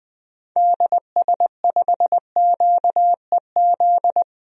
Morse code, heard as a keyed tone, about 20 wpm, 700 Hz. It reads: DS5QEZ
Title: DXCC-100 Call Signs - 6 Characters - 20wpm